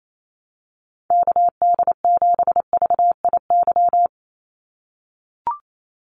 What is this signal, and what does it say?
Morse code, heard as a keyed tone, about 28 wpm, 700 Hz. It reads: KD7VIY
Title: USA Call Signs 28wpm